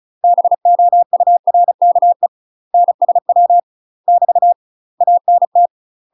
Morse code, heard as a keyed tone, about 34 wpm, 700 Hz. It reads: BOURKE NSW <BT> ANT
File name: UK MASTER QSOs TRAINING / 35WPM / UK MASTER QSOs 0000